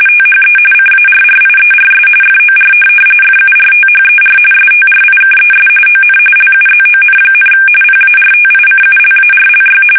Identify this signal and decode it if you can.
RTTY, about 45.45 baud. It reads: THE CARRIERS FOR FURTHER STRIKES ON LUZON, PARTICULARLY
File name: fsk850 / USS-Massachusetts